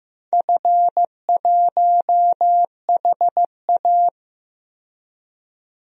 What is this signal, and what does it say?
Morse code, heard as a keyed tone, about 15 wpm, 700 Hz. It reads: F1HA
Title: DXCC-100 Call Signs - 4 Characters - 15wpm